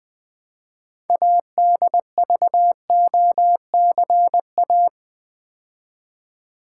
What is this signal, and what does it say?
Morse code, heard as a keyed tone, about 20 wpm, 700 Hz. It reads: ADVOCA
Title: Build A Word - 2 Letters At A Time - 8-Letter Words - 20wpm